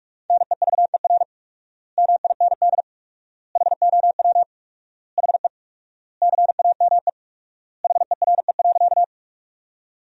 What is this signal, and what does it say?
Morse code, heard as a keyed tone, about 41 wpm, 700 Hz. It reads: NEVER MIND HOW HE CAME HERE.
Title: 4x Word-Spacing - Sentences from Top 500 Words - 45wpm